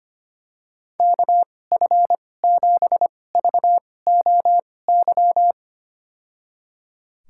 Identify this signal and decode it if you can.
Morse code, heard as a keyed tone, about 25 wpm, spaced as for 18 wpm, 700 Hz. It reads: KF7VOY
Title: USA Call Signs - 2x Character Spacing - 25wpm